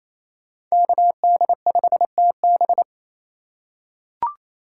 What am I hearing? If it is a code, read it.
Morse code, heard as a keyed tone, about 28 wpm, 700 Hz. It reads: KD5TB
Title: USA Call Signs - 5 Characters - 2x Character Spacing - 28wpm